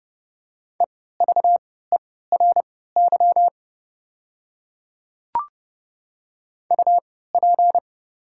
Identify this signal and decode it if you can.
Morse code, heard as a keyed tone, about 30 wpm, spaced as for 17 wpm, 690 Hz. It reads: EVERY UP
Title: 3x Character Spacing - Top 200 Words - 30wpm